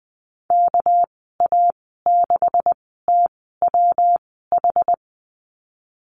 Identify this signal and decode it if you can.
Morse code, heard as a keyed tone, about 19 wpm, spaced as for 14 wpm, 700 Hz. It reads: KA6TWH
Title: 200 Lesson 32g - Letter k - Call Signs 20wpm